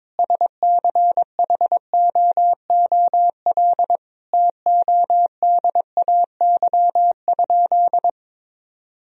Morse code, heard as a keyed tone, about 22 wpm, 700 Hz. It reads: SCHOOL TODAY?
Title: Sentences from Top 1000 Words - Review Part 4 - 22wpm